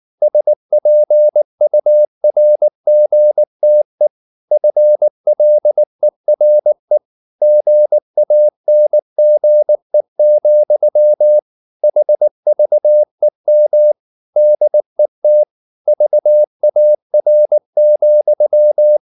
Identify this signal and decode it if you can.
Morse code, heard as a keyed tone, about 19 wpm, 600 Hz. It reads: SPURGTE FLERE GANGE, HVEM DET VAR,
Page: Ferie 19 wpm